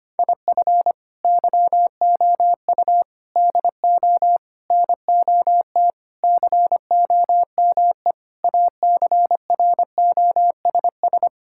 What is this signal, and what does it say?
Morse code, heard as a keyed tone, about 25 wpm, 700 Hz. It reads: IF YOU DO NOT COME ACROSS
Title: Sentences from Top 1000 Words - Review Part 5 - 25wpm